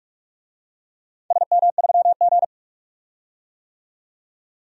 Morse code, heard as a keyed tone, about 46 wpm, 700 Hz. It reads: SM3G
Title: DXCC-100 Call Signs - 4 Characters - 45wpm